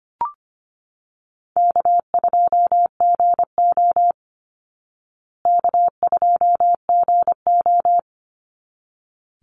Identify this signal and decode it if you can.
Morse code, heard as a keyed tone, about 25 wpm, 700 Hz. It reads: K2GO K2GO
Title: Easy Call Signs Repeated 2x - 25wpm